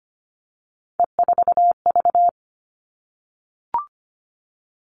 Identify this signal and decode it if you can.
Morse code, heard as a keyed tone, about 25 wpm, 700 Hz. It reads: E4V
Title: All Cut Numbers - 3-Digit 25wpm